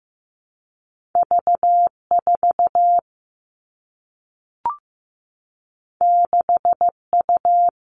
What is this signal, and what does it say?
Morse code, heard as a keyed tone, about 15 wpm, 700 Hz. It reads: V4 6U
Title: All Cut Numbers - 2-Digit 15wpm